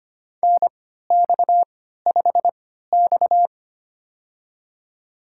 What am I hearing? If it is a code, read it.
Morse code, heard as a keyed tone, about 25 wpm, spaced as for 14 wpm, 700 Hz. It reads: NX5X
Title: USA Call Signs - 4 Characters - 3x Character Spacing - 25wpm